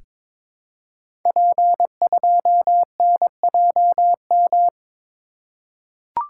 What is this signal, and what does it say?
Morse code, heard as a keyed tone, about 22 wpm, 700 Hz. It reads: P2NJM
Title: Code Groups - 5 Letter:Numbers - 22wpm